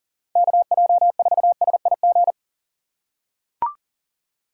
Morse code, heard as a keyed tone, about 40 wpm, 700 Hz. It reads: KJ4SIG
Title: USA Call Signs - 6 Characters - 6x Character Spacing - 40wpm